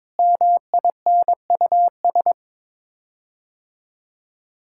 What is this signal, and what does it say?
Morse code, heard as a keyed tone, about 22 wpm, 700 Hz. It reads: MINUS
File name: practice / 22wpm / Sets of 8 Words - 22wpm